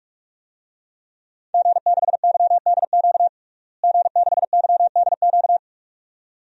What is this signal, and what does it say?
Morse code, heard as a keyed tone, about 46 wpm, 700 Hz. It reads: G6YDQ G6YDQ
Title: DXCC-100 Call Signs - 5 Characters - Repeated 2x - 45wpm